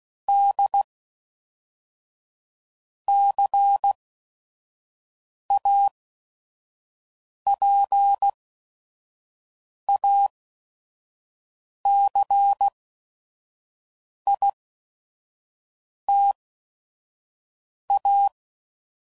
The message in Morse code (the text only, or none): D C A P A C I T A